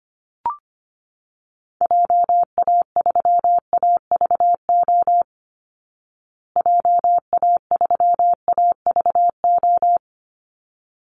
Morse code, heard as a keyed tone, about 25 wpm, 700 Hz. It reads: JA3AVO JA3AVO
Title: Call Signs Repeated 3x - 25wpm